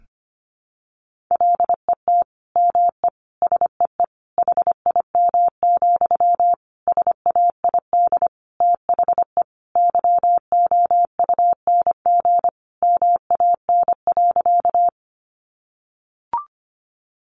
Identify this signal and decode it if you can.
Morse code, heard as a keyed tone, about 25 wpm, 700 Hz. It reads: LET ME SEE HIM, SAID THE YOUNG MAN.
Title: Sentences from Top 400 Words 25wpm